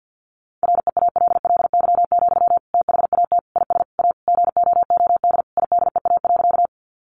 Morse code, heard as a keyed tone, about 50 wpm, 700 Hz. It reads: REALLY, THAT IS A GOOD IDEA.
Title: Sentences from Top 600 Words 50wpm